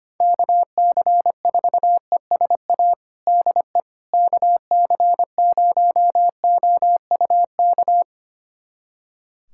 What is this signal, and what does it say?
Morse code, heard as a keyed tone, about 25 wpm, 700 Hz. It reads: KC4ESA DE KC0OUK
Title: USA Callsign DE Callsign - 6 Characters - 25wpm